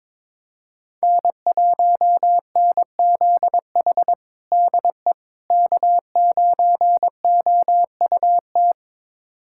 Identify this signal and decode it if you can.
Morse code, heard as a keyed tone, about 22 wpm, 700 Hz. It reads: N1NZH DE K9OUT
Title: USA Callsign DE Callsign - 5 Characters - 22wpm